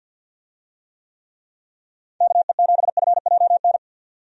Morse code, heard as a keyed tone, about 52 wpm, 700 Hz. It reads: KE7FJN
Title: USA Call Signs - 6 Characters - 50wpm